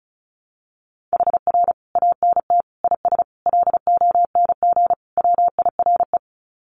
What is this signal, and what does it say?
Morse code, heard as a keyed tone, about 34 wpm, 700 Hz. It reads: HR ANT IS LONG WIRE
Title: Common QSO Phrases 35wpm